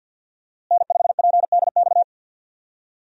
Morse code, heard as a keyed tone, about 47 wpm, 700 Hz. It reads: N5PDX
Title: USA Call Signs - 5 Characters - Repeated 3x - 50wpm